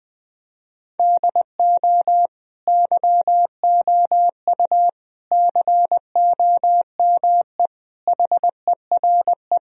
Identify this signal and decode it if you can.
Morse code, heard as a keyed tone, about 20 wpm, 700 Hz. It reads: DO YOU COME HERE
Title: Sentences from Top 1000 Words - Review Part 1 - 20wpm